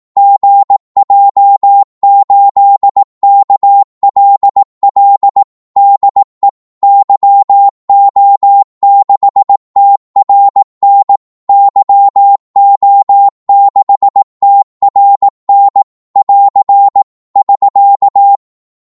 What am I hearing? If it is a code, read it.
Morse code, heard as a keyed tone, about 18 wpm, 800 Hz. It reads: GJ8KLL DE YO6TRN YO6TRN <AR> <SK>
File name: Morse / qso-18wpm